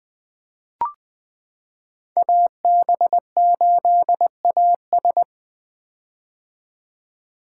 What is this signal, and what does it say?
Morse code, heard as a keyed tone, about 20 wpm, 700 Hz. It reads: AB8AS
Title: 2023 POTA Activator Call Signs - 20wpm